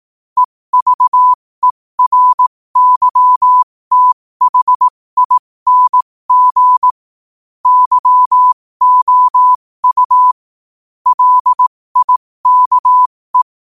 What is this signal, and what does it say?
Morse code, heard as a keyed tone, about 18 wpm, spaced as for 15 wpm, 1000 Hz. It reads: EVERYTHING YOU LIKE
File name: qotdfiles / 15wpm